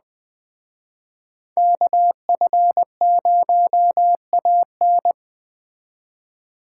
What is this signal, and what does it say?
Morse code, heard as a keyed tone, about 20 wpm, 700 Hz. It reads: KF0AN
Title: Speed Racing - USA 2x2 Call Signs 17wpm